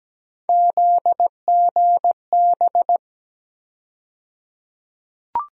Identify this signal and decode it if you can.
Morse code, heard as a keyed tone, about 17 wpm, 700 Hz. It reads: ZGB
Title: USA Call Sign Postfixes - 3 Letters 17wpm